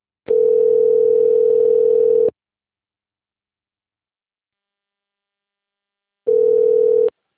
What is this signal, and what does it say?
Ringback tone (North America), 440 + 480 Hz.
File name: rg-1600-unknown- - - .84373